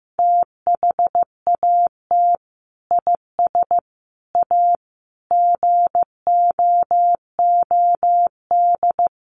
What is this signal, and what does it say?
Morse code, heard as a keyed tone, about 15 wpm, 700 Hz. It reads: THAT IS A GOOD
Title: Sentences from Top 700 Words 15wpm